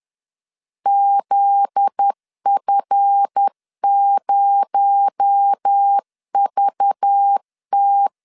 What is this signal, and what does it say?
Morse code, heard as a keyed tone, about 10 wpm, 800 Hz. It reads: ZF0VT
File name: LEZIONI / 7 GF9Z / 7 GF9Z 10wpm 2611